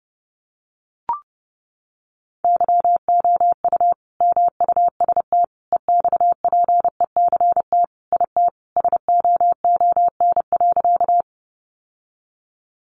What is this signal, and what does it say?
Morse code, heard as a keyed tone, about 30 wpm, 700 Hz. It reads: YOU MUST EXPECT IT SOON.
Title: Sentences from Top 1100 Words 30wpm